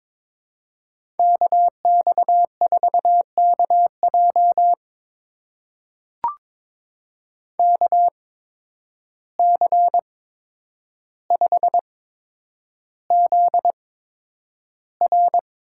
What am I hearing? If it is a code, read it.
Morse code, heard as a keyed tone, about 21 wpm, spaced as for 9 wpm, 700 Hz. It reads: KX4KJ KC5ZR
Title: USA Call Signs - 5 Characters - 8x Character Spacing - 22wpm